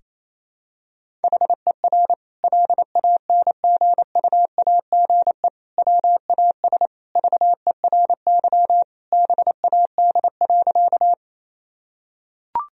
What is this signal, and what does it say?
Morse code, heard as a keyed tone, about 28 wpm, 700 Hz. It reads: HER LANGUAGE WAS VERY BAD.
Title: Sentences from Top 1000 Words - Review Part 3 - 28wpm